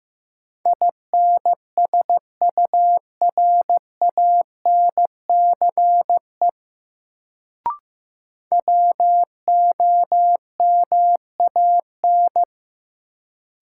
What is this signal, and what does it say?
Morse code, heard as a keyed tone, about 16 wpm, 700 Hz. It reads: INSURANCE WOMAN
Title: Sentences from Top 1100 Words - New Word Drill Alt - 15wpm